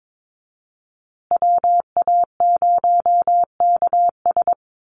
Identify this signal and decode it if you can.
Morse code, heard as a keyed tone, about 22 wpm, 700 Hz. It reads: WA0KS
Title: USA Call Signs Repeated 3x 22wpm